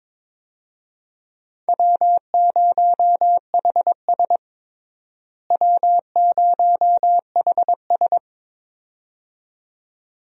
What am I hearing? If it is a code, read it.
Morse code, heard as a keyed tone, about 22 wpm, 700 Hz. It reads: W0HS W0HS
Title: USA Call Signs - 4 Characters - Repeated 2x - 22wpm